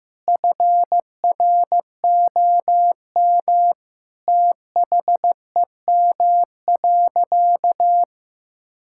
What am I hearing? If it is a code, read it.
Morse code, heard as a keyed tone, about 15 wpm, 700 Hz. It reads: FROM THEM.
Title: Sentences from Top 1600 Words 15wpm